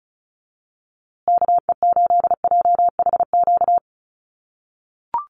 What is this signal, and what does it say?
Morse code, heard as a keyed tone, about 34 wpm, 700 Hz. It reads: KE8JHQ
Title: USA Call Signs Repeated 2x 35wpm